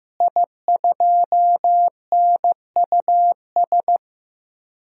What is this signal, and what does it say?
Morse code, heard as a keyed tone, about 15 wpm, 700 Hz. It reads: I2NUS
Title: DXCC-100 Call Signs - 5 Characters - Repeated 2x - 15wpm